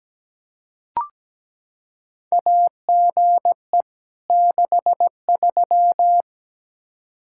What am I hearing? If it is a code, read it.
Morse code, heard as a keyed tone, about 17 wpm, 700 Hz. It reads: AGE 63
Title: Practice QSO - Exchange 3 - More Info - 17wpm